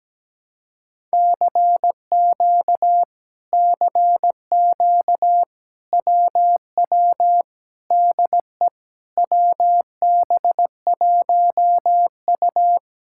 Morse code, heard as a keyed tone, about 17 wpm, 700 Hz. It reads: CQ CQ WW DE WB1U
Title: Speed-Racing Common QSO Phrases - 17wpm